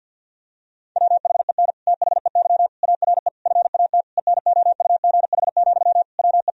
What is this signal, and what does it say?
Morse code, heard as a keyed tone, about 50 wpm, 700 Hz. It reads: WHEN THEY ARE FAT ENOUGH, WE
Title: Sentences from Top 1000 Words - Review Part 4 - 50wpm